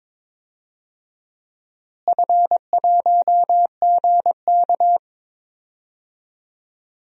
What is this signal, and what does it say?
Morse code, heard as a keyed tone, about 22 wpm, 700 Hz. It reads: F1GK
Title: DXCC-100 Call Signs - 4 Characters - Repeated 3x - 22wpm